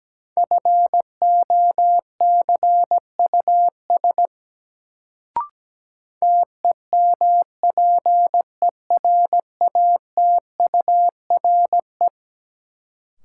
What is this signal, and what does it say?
Morse code, heard as a keyed tone, about 17 wpm, 700 Hz. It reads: FOCUS TEMPERATURE
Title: Sentences from Top 1000 Words - Review Part 4 - New Word Drill - 17wpm